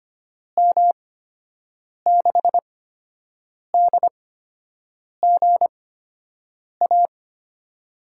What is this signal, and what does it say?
Morse code, heard as a keyed tone, about 25 wpm, spaced as for 7 wpm, 700 Hz. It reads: M6DGA
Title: USA Call Signs - 6 Characters - 8x Character Spacing - 25wpm